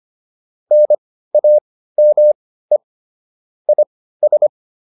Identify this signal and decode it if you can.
Morse code, heard as a keyed tone, about 25 wpm, spaced as for 15 wpm, 600 Hz. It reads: NAME IS
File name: QSO401 15